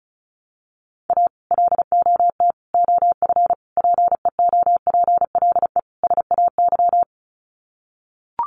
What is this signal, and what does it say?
Morse code, heard as a keyed tone, about 34 wpm, 700 Hz. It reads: A LOT OF PEOPLE SAY
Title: Sets of 5 Words Encore2 35wpm